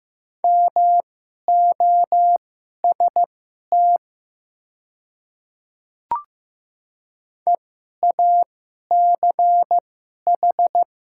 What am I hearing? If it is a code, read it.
Morse code, heard as a keyed tone, about 15 wpm, spaced as for 11 wpm, 700 Hz. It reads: MOST EACH
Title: 2x Character Spacing - Top 300 Words - 15wpm